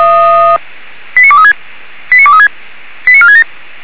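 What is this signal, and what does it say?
Five-tone selective call (ZVEI), heard as quick runs of five tones, tones of about 70 ms. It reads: 89236 89236 89467
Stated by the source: ZVEI five-tone selective call received on an SDR.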